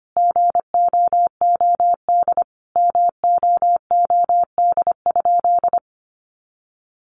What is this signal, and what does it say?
Morse code, heard as a keyed tone, about 25 wpm, 700 Hz. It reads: GOOD MOOD?
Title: Sentences from Top 1100 Words 25wpm